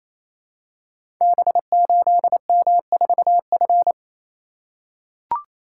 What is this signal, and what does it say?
Morse code, heard as a keyed tone, about 27 wpm, 700 Hz. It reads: B8M4F
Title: Code Groups - 5 Letter:Numbers - 28wpm